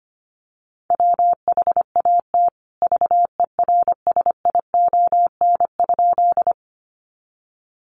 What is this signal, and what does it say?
Morse code, heard as a keyed tone, about 25 wpm, 700 Hz. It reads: WHAT VERSION?
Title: Sentences from Top 1400 Words 25wpm